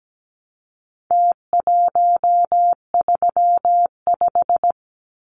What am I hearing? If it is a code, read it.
Morse code, heard as a keyed tone, about 17 wpm, 700 Hz. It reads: T135
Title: Numbers - Contest Style - 4-Digit 17wpm